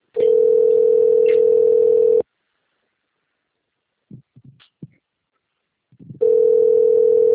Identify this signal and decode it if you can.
Ringback tone (North America), 440 + 480 Hz.